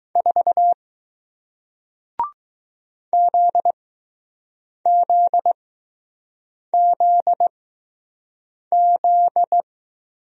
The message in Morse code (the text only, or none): 4 ZZZZ